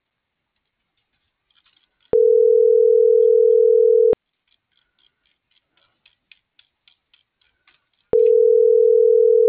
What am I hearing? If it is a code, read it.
Ringback tone (North America), 440 + 480 Hz.